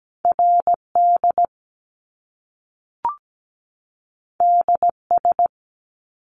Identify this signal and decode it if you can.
Morse code, heard as a keyed tone, about 17 wpm, 700 Hz. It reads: RD DS
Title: Top Two Letter Combinations in Words 17wpm